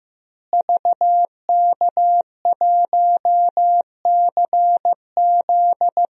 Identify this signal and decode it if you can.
Morse code, heard as a keyed tone, about 16 wpm, 700 Hz. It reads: VK1CZ
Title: DXCC-100 Call Signs - 5 Characters - Repeated 3x - 15wpm